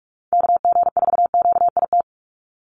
Morse code, heard as a keyed tone, about 44 wpm, 700 Hz. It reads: KG4QIT
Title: USA Call Signs 45wpm